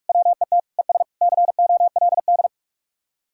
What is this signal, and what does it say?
Morse code, heard as a keyed tone, about 47 wpm, 700 Hz. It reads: WET ES COLD